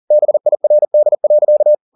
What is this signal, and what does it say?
Morse code, heard as a keyed tone, about 40 wpm, 600 Hz. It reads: BIRD.